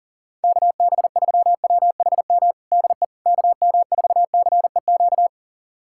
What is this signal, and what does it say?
Morse code, heard as a keyed tone, about 40 wpm, 700 Hz. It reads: KB3WHM DE KM4CEQ
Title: USA Callsign DE Callsign - 6 Characters - 40wpm